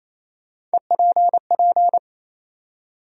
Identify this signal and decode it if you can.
Morse code, heard as a keyed tone, about 28 wpm, 700 Hz. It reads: EPP
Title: USA Call Sign Postfixes - 3 Letters 28wpm